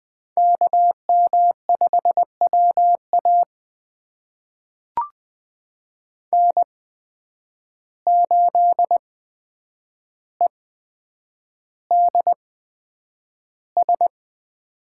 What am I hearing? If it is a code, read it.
Morse code, heard as a keyed tone, about 20 wpm, spaced as for 9 wpm, 700 Hz. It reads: KM5WA N8EDS
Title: USA Call Signs - 5 Characters - 8x Character Spacing - 20wpm